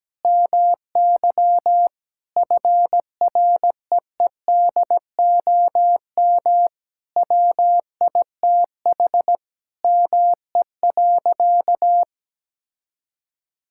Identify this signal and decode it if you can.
Morse code, heard as a keyed tone, about 17 wpm, 700 Hz. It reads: MY FREEDOM WITH ME.
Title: Sentences from Top 1000 Words - Review Part 5 - 17wpm